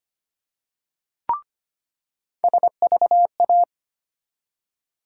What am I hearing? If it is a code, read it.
Morse code, heard as a keyed tone, about 25 wpm, 700 Hz. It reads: SVA
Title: USA Call Sign Postfixes - 3 Letters 25wpm